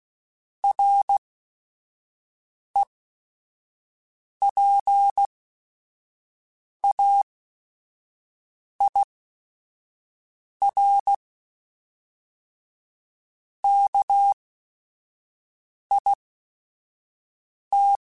Morse code, heard as a keyed tone, about 16 wpm, spaced as for 5 wpm, 800 Hz. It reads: REPAIR KIT